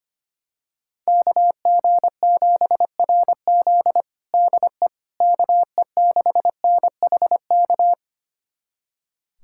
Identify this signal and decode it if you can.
Morse code, heard as a keyed tone, about 25 wpm, 700 Hz. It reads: KG7RZ DE KE6NHK
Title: USA Callsign DE Callsign 25wpm